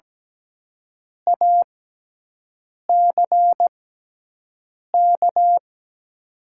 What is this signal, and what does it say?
Morse code, heard as a keyed tone, about 17 wpm, spaced as for 6 wpm, 700 Hz. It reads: ACK
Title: 6x Character Spacing - Top 300 Words - 17wpm